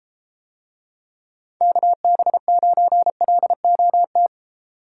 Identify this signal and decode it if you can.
Morse code, heard as a keyed tone, about 33 wpm, 700 Hz. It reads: KB9LOT